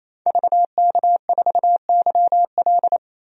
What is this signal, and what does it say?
Morse code, heard as a keyed tone, about 27 wpm, 700 Hz. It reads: VK4YL
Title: DXCC-100 Call Signs - 5 Characters - 28wpm